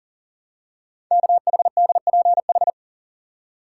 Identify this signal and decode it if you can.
Morse code, heard as a keyed tone, about 41 wpm, 690 Hz. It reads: KHDPH
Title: Code Groups - 5 Letters - 40wpm